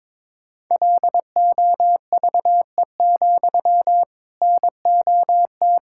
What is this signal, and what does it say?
Morse code, heard as a keyed tone, about 22 wpm, 700 Hz. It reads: LOVE, NOT
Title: Sentences from Top 1600 Words 22wpm